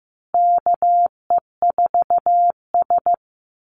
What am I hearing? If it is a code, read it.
Morse code, heard as a keyed tone, about 15 wpm, 700 Hz. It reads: KE4S
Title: Call Signs Repeated 3x - 15wpm